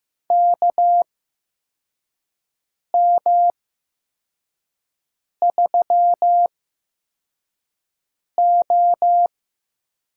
Morse code, heard as a keyed tone, about 15 wpm, spaced as for 4 wpm, 700 Hz. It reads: KM3O
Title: USA Call Signs - 8x Character Spacing - 15wpm